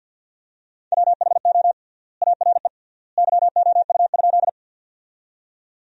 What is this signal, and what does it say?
Morse code, heard as a keyed tone, about 50 wpm, 700 Hz. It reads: WHO ARE YOU?
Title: 3x Word-Spacing - Sentences from Top 100 Words - 50wpm